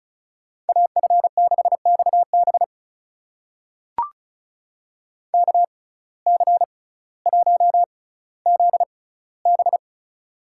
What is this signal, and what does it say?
Morse code, heard as a keyed tone, about 36 wpm, spaced as for 18 wpm, 700 Hz. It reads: AF6XB KC1ZB
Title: USA Call Signs - 5 Characters - 6x Character Spacing - 35wpm